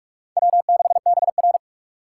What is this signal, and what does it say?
Morse code, heard as a keyed tone, about 44 wpm, 690 Hz. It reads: W6BR
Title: USA Call Signs - 4 Characters - 2x Character Spacing - 45wpm